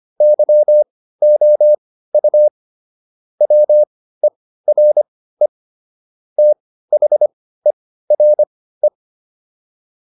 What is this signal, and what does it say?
Morse code, heard as a keyed tone, about 25 wpm, spaced as for 15 wpm, 600 Hz. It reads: YOU WERE THERE